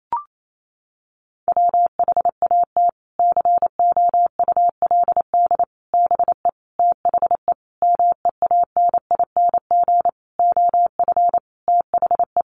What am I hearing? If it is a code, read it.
Morse code, heard as a keyed tone, about 28 wpm, 700 Hz. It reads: WHAT COULD BE THE MEANING OF THE
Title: Sentences from Top 900 Words 28wpm